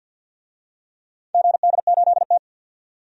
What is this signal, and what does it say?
Morse code, heard as a keyed tone, about 50 wpm, 700 Hz. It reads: GD8T
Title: DXCC-100 Call Signs - 4 Characters - 50wpm